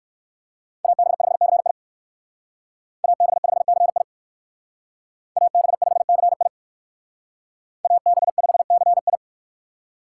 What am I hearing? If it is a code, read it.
Morse code, heard as a keyed tone, about 59 wpm, 700 Hz. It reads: AB5CI AB5CI AB5CI AB5CI